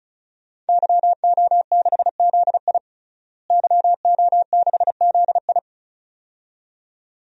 Morse code, heard as a keyed tone, about 34 wpm, 700 Hz. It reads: YO6ZI YO6ZI
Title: DXCC-100 Call Signs - 5 Characters - Repeated 2x - 35wpm